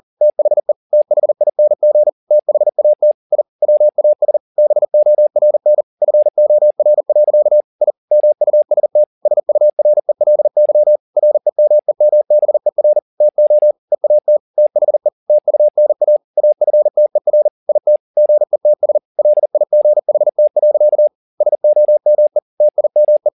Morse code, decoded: THE THING THAT I WAS BORN FOR. I MUST SURELY REMEMBER TO EAT THE TUNA AFTER IT GETS LIGHT. SOME TIME

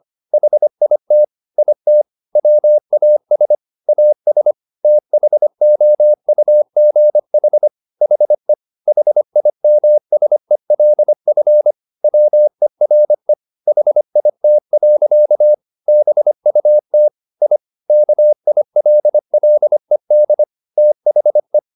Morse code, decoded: HIT IT WAS AS THOUGH HE HIMSELF WERE HIT. BUT I KILLED THE

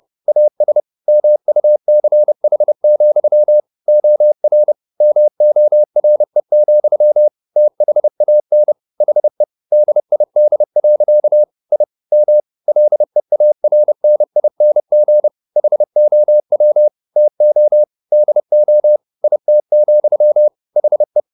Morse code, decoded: AS MUCH, OR MORE, THAN HE DID. I M LEARNING HOW TO DO IT, HE

